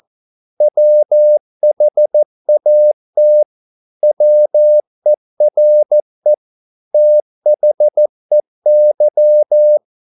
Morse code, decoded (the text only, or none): WHAT WERE THEY